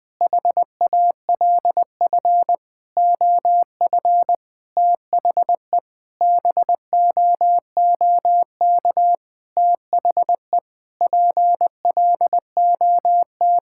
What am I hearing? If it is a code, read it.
Morse code, heard as a keyed tone, about 20 wpm, 700 Hz. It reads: HALF OF THE BOOK THE PLOT
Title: Sentences with 5-Word Phrases - Spoken Beginning - 20wpm